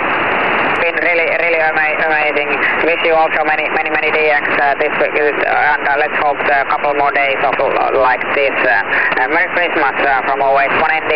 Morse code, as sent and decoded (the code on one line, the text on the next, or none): none